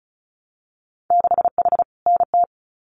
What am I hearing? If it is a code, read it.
Morse code, heard as a keyed tone, about 34 wpm, 700 Hz. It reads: 6H NT
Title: Winter Field Day Exchanges Encore 35wpm